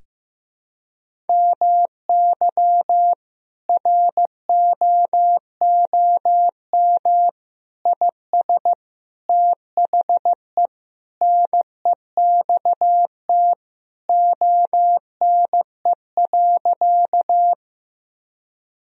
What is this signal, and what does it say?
Morse code, heard as a keyed tone, about 15 wpm, 700 Hz. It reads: MY ROOM IS THE NEXT ONE.
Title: Sentences from Top 1000 Words - Review Part 1 - 15wpm